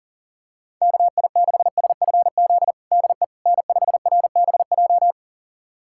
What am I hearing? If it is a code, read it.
Morse code, heard as a keyed tone, about 40 wpm, 700 Hz. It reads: KI6SFZ DE N5RBJ
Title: USA Callsign DE Callsign 40wpm